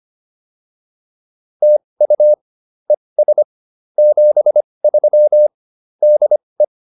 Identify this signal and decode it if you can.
Morse code, heard as a keyed tone, about 25 wpm, spaced as for 20 wpm, 600 Hz. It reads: TU ES 73 DE